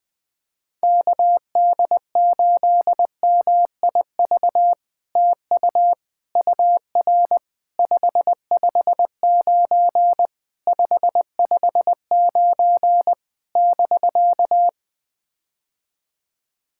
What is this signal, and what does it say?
Morse code, heard as a keyed tone, about 21 wpm, 700 Hz. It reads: KD8MIV TU UR 559 559 <BK>
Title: POTA QSO Practice - 20wpm